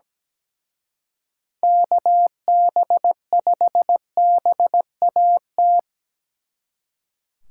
Morse code, heard as a keyed tone, about 20 wpm, 700 Hz. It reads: KB5BAT
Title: Speed Racing - USA Call Signs 17wpm